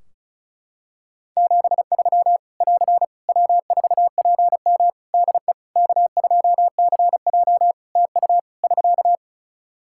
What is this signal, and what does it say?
Morse code, heard as a keyed tone, about 36 wpm, 700 Hz. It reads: 73 <AR> W4PM DE K2CJ TU <SK>
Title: K1USN SST QSO Practice 35wpm